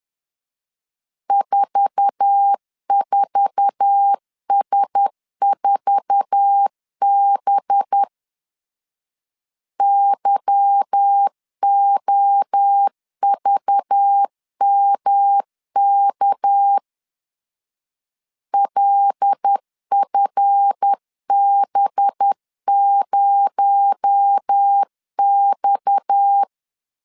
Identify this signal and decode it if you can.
Morse code, heard as a keyed tone, about 10 wpm, 800 Hz. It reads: 44S4B YOVMK LFB0X
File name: LEZIONI / 8 YJ45 / 8 YJ4 10wpm 2633